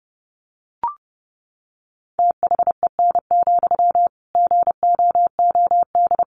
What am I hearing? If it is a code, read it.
Morse code, heard as a keyed tone, about 30 wpm, 700 Hz. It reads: THEN, GOOD
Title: Sentences from Top 300 Words 30wpm